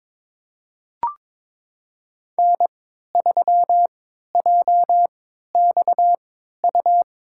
Morse code, heard as a keyed tone, about 21 wpm, spaced as for 12 wpm, 700 Hz. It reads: N3JXU
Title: USA Call Signs - 5 Characters - 3x Character Spacing - 22wpm